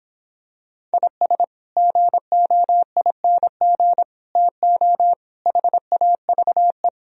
Morse code, heard as a keyed tone, about 26 wpm, 700 Hz. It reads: IS GOING TO HAVE
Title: Speed-Racing - Sets of 5 Words 22wpm